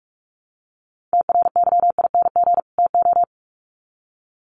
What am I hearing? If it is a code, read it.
Morse code, heard as a keyed tone, about 44 wpm, 700 Hz. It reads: TRYING TO
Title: Sets of 2 Words 45wpm